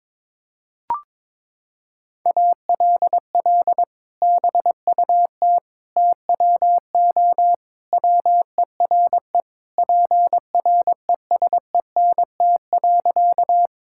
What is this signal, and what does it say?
Morse code, heard as a keyed tone, about 22 wpm, 700 Hz. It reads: ALL BUT TWO WERE PRESENT.
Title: Sentences from Top 1000 Words - Review Part 5 - 22wpm